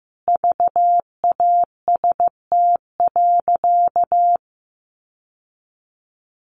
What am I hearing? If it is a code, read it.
Morse code, heard as a keyed tone, about 15 wpm, 700 Hz. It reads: VAST.
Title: Sentences from Top 1300 Words 15wpm